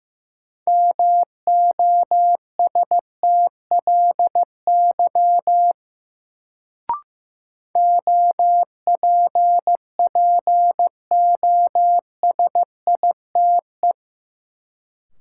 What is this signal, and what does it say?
Morse code, heard as a keyed tone, about 15 wpm, 700 Hz. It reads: MOSTLY OPPOSITE
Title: Sentences from Top 1300 Words - New Word Drill Alt - 15wpm